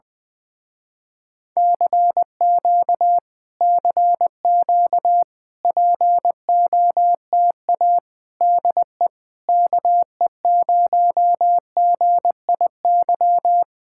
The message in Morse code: CQ CQ POTA DE KE0GIY

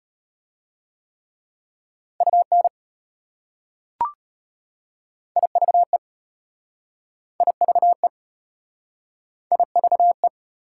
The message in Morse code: UN IVE IVE IVE

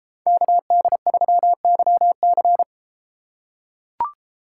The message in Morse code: KD3YC

